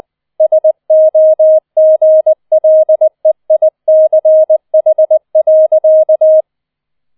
Morse code, heard as a keyed tone, about 20 wpm, 630 Hz. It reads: SOGLEICH.